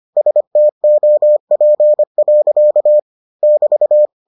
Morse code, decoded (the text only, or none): STOP. <BT>